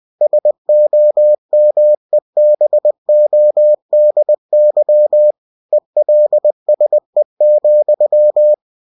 Morse code, heard as a keyed tone, about 20 wpm, 600 Hz. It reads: SOMEBODY ELSE,
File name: cw-books / winnie-the-pooh / winnie-20wpm-page-0013